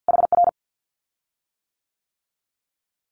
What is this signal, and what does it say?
Morse code, heard as a keyed tone, about 50 wpm, 700 Hz. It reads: HR
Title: Common QSO Phrases Encore 50wpm